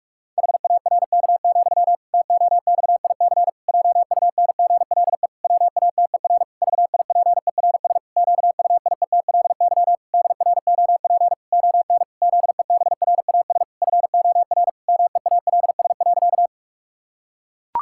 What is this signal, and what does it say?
Morse code, heard as a keyed tone, about 41 wpm, 700 Hz. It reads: HARK, TOXIC JUNGLE WATER VIPERS QUIETLY DROP ON ZEBRAS FOR MEALS.
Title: Pangram Challenge v2 - 45wpm